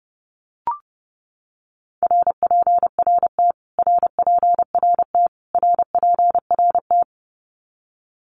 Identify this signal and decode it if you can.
Morse code, heard as a keyed tone, about 30 wpm, 700 Hz. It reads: RPRT RPRT RPRT
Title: Abbreviations Repeated 3x - 30wpm